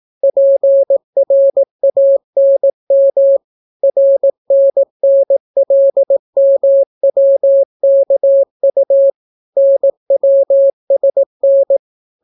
Morse code, heard as a keyed tone, about 18 wpm, 550 Hz. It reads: PRANM RNNLMWKU NWSN